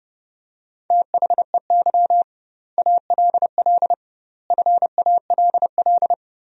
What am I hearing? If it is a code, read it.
Morse code, heard as a keyed tone, about 32 wpm, 700 Hz. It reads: THEY ALL FALL
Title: 2x Word-Spacing - Sentences from Top 500 Words - 30wpm